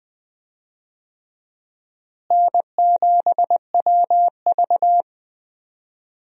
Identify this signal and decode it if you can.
Morse code, heard as a keyed tone, about 21 wpm, 700 Hz. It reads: N7WV
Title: USA Call Signs - 4 Characters - Repeated 2x - 20wpm